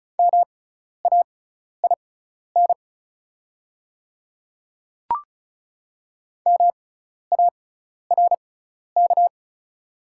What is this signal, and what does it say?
Morse code, heard as a keyed tone, about 36 wpm, spaced as for 12 wpm, 690 Hz. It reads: MAIN MARK
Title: 6x Character Spacing - Top 300 Words - 35wpm